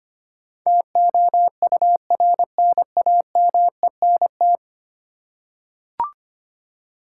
Morse code, heard as a keyed tone, about 25 wpm, 700 Hz. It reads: TOURNAMENT
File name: Ten Letter Words - 25wpm